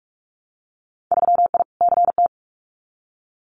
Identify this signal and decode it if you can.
Morse code, heard as a keyed tone, about 44 wpm, 700 Hz. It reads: 3I CT
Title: Winter Field Day Exchanges 45wpm